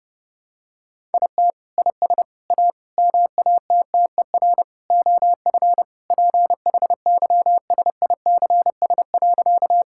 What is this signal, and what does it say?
Morse code, heard as a keyed tone, about 32 wpm, 700 Hz. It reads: IT IS A MATTER OF PHYSICS.
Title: Sentences from Top 1000 Words - Review Part 5 - 30wpm